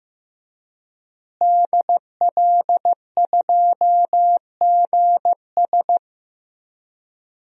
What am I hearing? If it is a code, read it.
Morse code, heard as a keyed tone, about 15 wpm, 700 Hz. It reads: DL2GS
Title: DXCC-100 Call Signs - 5 Characters - Repeated 3x - 15wpm